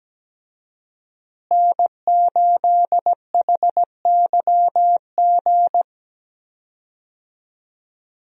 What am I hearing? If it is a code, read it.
Morse code, heard as a keyed tone, about 17 wpm, 700 Hz. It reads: N8HYG
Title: USA Call Signs - 5 Characters - Repeated 2x - 17wpm